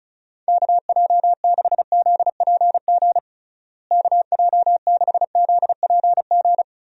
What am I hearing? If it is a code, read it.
Morse code, heard as a keyed tone, about 36 wpm, 700 Hz. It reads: KJ6ZPG KJ6ZPG
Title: USA Call Signs - 6 Characters - Repeated 2x - 35wpm